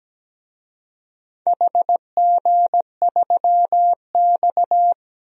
Morse code, heard as a keyed tone, about 17 wpm, 700 Hz. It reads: HG3X
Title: DXCC-100 Call Signs - 4 Characters - Repeated 2x - 17wpm